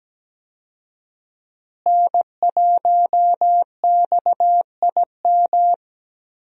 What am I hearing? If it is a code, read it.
Morse code, heard as a keyed tone, about 17 wpm, 700 Hz. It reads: N1XIM
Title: USA Call Signs - 5 Characters - Repeated 2x - 17wpm